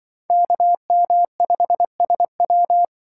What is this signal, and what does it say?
Morse code, heard as a keyed tone, about 24 wpm, 700 Hz. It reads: KM5SW
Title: Speed Racing - USA 2x2 Call Signs 20wpm